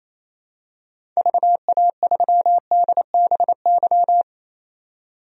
Morse code, heard as a keyed tone, about 27 wpm, 700 Hz. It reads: VA3DBY
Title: DXCC-100 Call Signs - 6 Characters - 28wpm